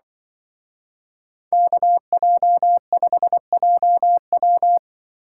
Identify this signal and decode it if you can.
Morse code, heard as a keyed tone, about 25 wpm, 700 Hz. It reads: KJ5JW
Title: Speed Racing - USA 2x2 Call Signs 20wpm